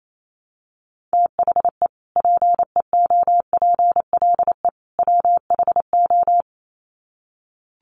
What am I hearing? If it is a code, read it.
Morse code, heard as a keyed tone, about 28 wpm, 700 Hz. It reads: THE PEOPLE WHO
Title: Sets of 3 Words 28wpm